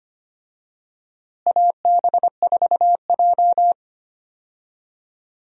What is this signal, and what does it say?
Morse code, heard as a keyed tone, about 25 wpm, 700 Hz. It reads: AB4J
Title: USA Call Signs - 4 Characters - Repeated 2x - 25wpm